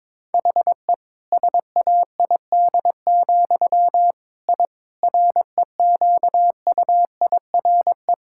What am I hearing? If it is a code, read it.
Morse code, heard as a keyed tone, about 22 wpm, 700 Hz. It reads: HE SAID, I REQUIRE
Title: Sentences from Top 1000 Words - Review Part 6 - 22wpm